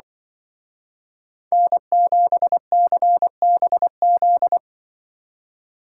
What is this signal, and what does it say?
Morse code, heard as a keyed tone, about 24 wpm, 700 Hz. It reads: N7CBZ
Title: Speed Racing - USA Call Signs 20wpm